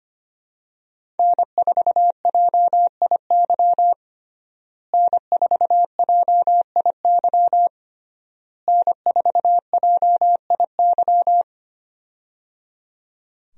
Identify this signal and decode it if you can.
Morse code, heard as a keyed tone, about 25 wpm, 700 Hz. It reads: N4JIY N4JIY N4JIY
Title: USA Call Signs - 5 Characters - Repeated 3x - 25wpm